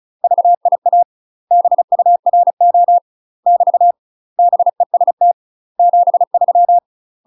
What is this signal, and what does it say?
Morse code, heard as a keyed tone, about 34 wpm, 700 Hz. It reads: VIA BURO <BT> BEST 73
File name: UK MASTER QSOs TRAINING / 35WPM / UK MASTER QSOs 0024